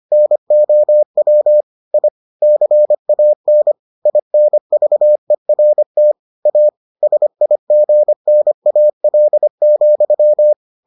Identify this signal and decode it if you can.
Morse code, heard as a keyed tone, about 25 wpm, 600 Hz. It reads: NOW I CAN INVERT A SIGNAL,